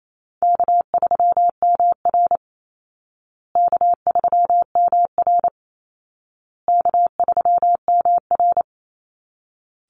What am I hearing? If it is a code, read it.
Morse code, heard as a keyed tone, about 28 wpm, 700 Hz. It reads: K3MR K3MR K3MR
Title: Easy Call Signs Repeated 3x - 28wpm